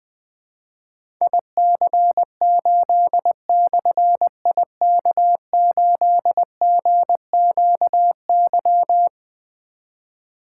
Morse code, heard as a keyed tone, about 20 wpm, 700 Hz. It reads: IC8/IK8GQY
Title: Difficult Call Signs 20wpm